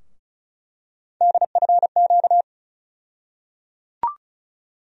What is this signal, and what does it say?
Morse code, heard as a keyed tone, about 34 wpm, 700 Hz. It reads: DFQ
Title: USA Call Sign Postfixes - 3 Letters 35wpm